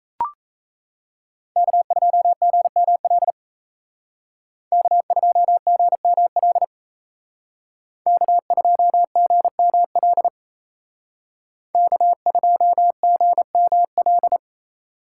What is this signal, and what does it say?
Morse code, heard as a keyed tone, about 35 wpm, 700 Hz. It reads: K2GML K2GML K2GML K2GML